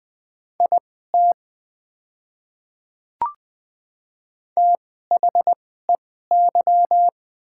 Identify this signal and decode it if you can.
Morse code, heard as a keyed tone, about 20 wpm, spaced as for 14 wpm, 690 Hz. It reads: IT THEY
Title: 2x Character Spacing - Top 100 Words - 20wpm